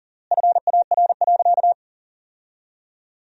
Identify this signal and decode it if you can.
Morse code, heard as a keyed tone, about 40 wpm, 700 Hz. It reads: FAR.
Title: 3x Word-Spacing - Sentences from Top 400 Words - 40wpm